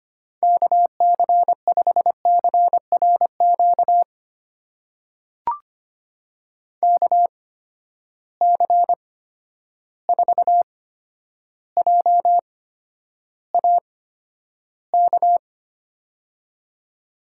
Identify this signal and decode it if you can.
Morse code, heard as a keyed tone, about 25 wpm, spaced as for 11 wpm, 700 Hz. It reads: KC5CRQ KC4JAK
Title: USA Call Signs - 6 Characters - 8x Character Spacing - 25wpm